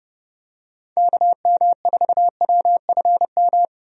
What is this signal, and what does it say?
Morse code, heard as a keyed tone, about 32 wpm, 700 Hz. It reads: KM4WFM
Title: USA Call Signs - 6 Characters - Repeated 3x - 30wpm